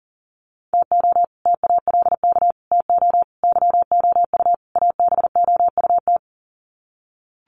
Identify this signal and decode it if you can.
Morse code, heard as a keyed tone, about 40 wpm, 700 Hz. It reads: TO TALK TO YOU ABOUT
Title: Sets of 5 Words Encore 40wpm